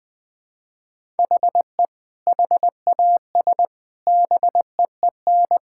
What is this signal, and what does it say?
Morse code, heard as a keyed tone, about 20 wpm, 700 Hz. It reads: HE HAS BEEN
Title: Sentences with 4-Word Phrases - 20wpm